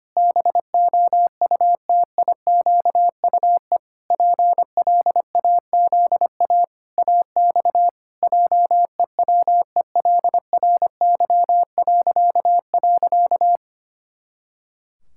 Morse code, heard as a keyed tone, about 25 wpm, 700 Hz. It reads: BOUTIQUE PLAZA AX JEWELRY..
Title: Pangram Challenge v2 - 25wpm